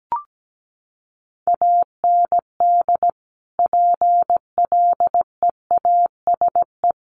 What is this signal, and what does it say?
Morse code, heard as a keyed tone, about 17 wpm, 700 Hz. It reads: AND PLEASE
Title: Sentences from Top 1500 Words 17wpm